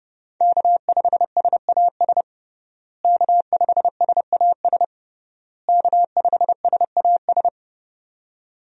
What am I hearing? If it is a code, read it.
Morse code, heard as a keyed tone, about 32 wpm, 700 Hz. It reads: K5SAS K5SAS K5SAS
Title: USA Call Signs - 5 Characters - Repeated 3x - 30wpm